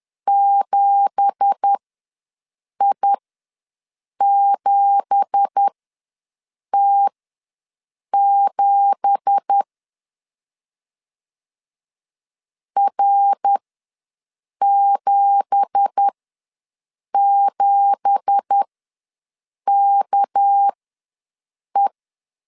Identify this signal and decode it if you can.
Morse code, heard as a keyed tone, about 10 wpm, spaced as for 6 wpm, 800 Hz. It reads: 7I7T7 R77KE
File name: LEZIONI / 9 2867 ws 10wpm 4215